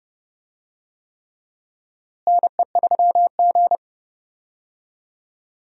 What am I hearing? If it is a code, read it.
Morse code, heard as a keyed tone, about 30 wpm, 700 Hz. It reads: NE3G
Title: USA Call Signs - 4 Characters - Repeated 3x - 30wpm